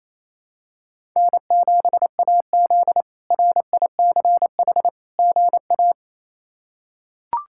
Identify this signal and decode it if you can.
Morse code, heard as a keyed tone, about 28 wpm, 700 Hz. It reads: N7AZ RICH GA
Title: K1USN SST QSO Practice - 28wpm